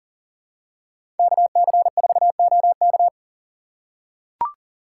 Morse code, heard as a keyed tone, about 40 wpm, 700 Hz. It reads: KC4OK
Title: USA Call Signs - 5 Characters - 2x Character Spacing - 40wpm